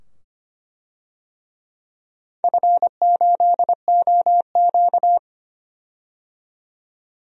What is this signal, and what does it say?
Morse code, heard as a keyed tone, about 25 wpm, 700 Hz. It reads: F8OQ
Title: DXCC-100 Call Signs - 4 Characters - 25wpm